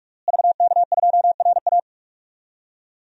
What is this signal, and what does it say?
Morse code, heard as a keyed tone, about 42 wpm, 700 Hz. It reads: VK2RA
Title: DXCC-100 Call Signs - 5 Characters - Repeated 3x - 45wpm